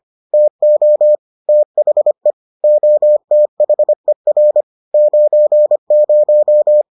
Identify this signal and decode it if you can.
Morse code, heard as a keyed tone, about 25 wpm, 600 Hz. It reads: TO THE OTHER 90